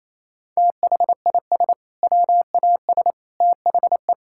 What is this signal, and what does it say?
Morse code, heard as a keyed tone, about 28 wpm, 700 Hz. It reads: THIS WAS THE
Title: Sentences from Top 1000 Words - Review Part 6 - 28wpm